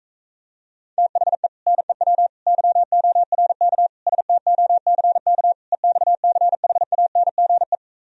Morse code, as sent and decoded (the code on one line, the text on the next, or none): - .... / -. .-- / -.-- --- .-. -.- / ... - --- -.-. -.- / . -..- -.-. .... .- -. --. .
THE NEW YORK STOCK EXCHANGE